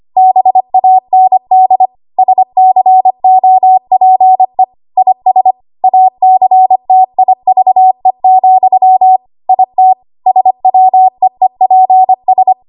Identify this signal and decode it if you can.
Morse code, heard as a keyed tone, about 25 wpm, 750 Hz. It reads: BAND SCOPE IS ACTIVE, IT SWEEPS
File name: Archive / 25 WPM / 25WPM